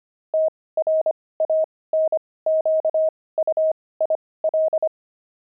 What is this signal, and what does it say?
Morse code, heard as a keyed tone, about 25 wpm, spaced as for 18 wpm, 630 Hz. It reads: TRANQUIL
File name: pangram-fs-18-cs-25-aardvark